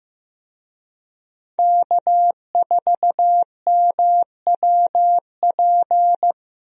Morse code, heard as a keyed tone, about 15 wpm, 700 Hz. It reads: K4MWP
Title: USA Call Signs - 5 Characters - Repeated 3x - 15wpm